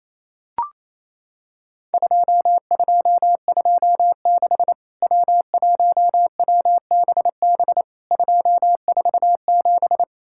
Morse code, heard as a keyed tone, about 28 wpm, 700 Hz. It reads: 2226 W1WBB 247
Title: Worked All Europe DX Contest - QTC Exchange - 28wpm